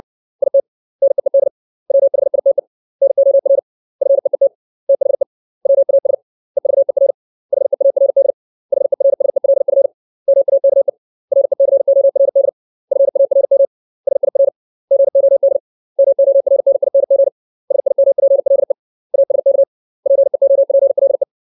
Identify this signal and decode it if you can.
Morse code, heard as a keyed tone, about 60 wpm, 550 Hz. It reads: IT NEED WHITE NOR FEET THE WAS EVER HEARD HERSELF MAKE RECORD FARM SIR GOD MORNING SIMPLE ASK PEOPLE
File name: 632 words / 632 Words 60WPM